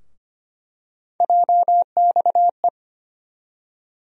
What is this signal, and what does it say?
Morse code, heard as a keyed tone, about 25 wpm, 700 Hz. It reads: JXE
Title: USA Call Sign Postfixes - 3 Letters 25wpm